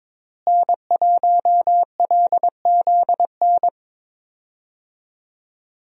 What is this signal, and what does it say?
Morse code, heard as a keyed tone, about 21 wpm, 700 Hz. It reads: N1LZN
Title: USA Call Signs - 5 Characters - Repeated 3x - 22wpm